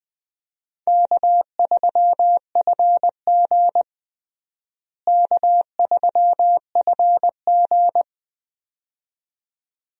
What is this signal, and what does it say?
Morse code, heard as a keyed tone, about 20 wpm, 700 Hz. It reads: K3FG K3FG
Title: USA Call Signs - 4 Characters - Repeated 2x - 20wpm